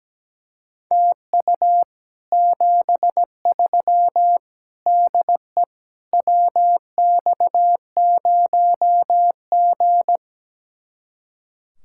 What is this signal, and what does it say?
Morse code, heard as a keyed tone, about 17 wpm, 700 Hz. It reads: TU 73 DE WX0G
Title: Speed-Racing Common QSO Phrases Encore - 17wpm